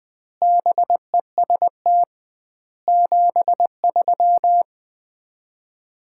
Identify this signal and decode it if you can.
Morse code, heard as a keyed tone, about 20 wpm, 700 Hz. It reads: BEST 73
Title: Common QSO Phrases - 2x Word-Spacing - 20wpm